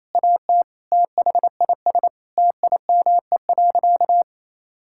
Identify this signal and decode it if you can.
Morse code, heard as a keyed tone, about 28 wpm, 700 Hz. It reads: AT THIS TIME.
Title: Sentences from Top 1000 Words - Review Part 5 - 28wpm